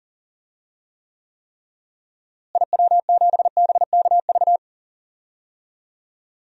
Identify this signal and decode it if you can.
Morse code, heard as a keyed tone, about 38 wpm, 700 Hz. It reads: IW7BKV
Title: DXCC-100 Call Signs - 6 Characters - Repeated 3x - 40wpm